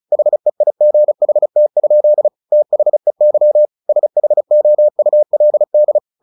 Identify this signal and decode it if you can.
Morse code, heard as a keyed tone, about 34 wpm, 600 Hz. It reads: HEIGHT? THEY SHOULD
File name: the old man and the sea 2 35wpm